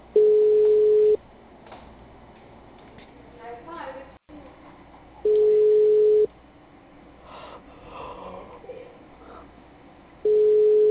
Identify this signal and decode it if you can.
Ringback tone (Europe, 425 Hz), with speech.
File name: out- -2030- - - .9153